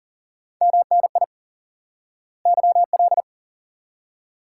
Morse code, heard as a keyed tone, about 40 wpm, 700 Hz. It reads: MNI YL
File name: IQE Course 3 - Lesson 89 - YAGI - Two Elements - 40wpm